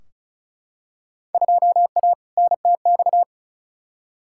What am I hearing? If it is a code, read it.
Morse code, heard as a keyed tone, about 34 wpm, 700 Hz. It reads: 2A NTX
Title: Field Day Exchanges - 35wpm